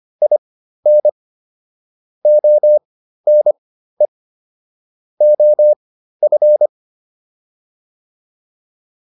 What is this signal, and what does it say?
Morse code, heard as a keyed tone, about 25 wpm, spaced as for 13 wpm, 600 Hz. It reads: IN ONE OF